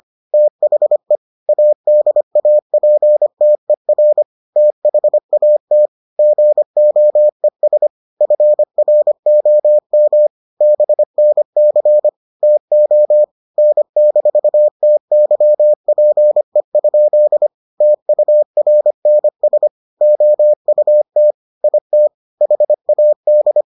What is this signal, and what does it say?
Morse code, heard as a keyed tone, about 25 wpm, 600 Hz. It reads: THE ADAPTER THAT GOES FROM BNC TO N-TYPE? TURNS OUT IT HAD